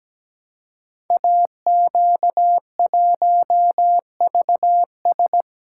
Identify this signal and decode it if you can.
Morse code, heard as a keyed tone, about 17 wpm, 700 Hz. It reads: AQ1VS
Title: Code Groups - 5 Letter:Numbers - 17wpm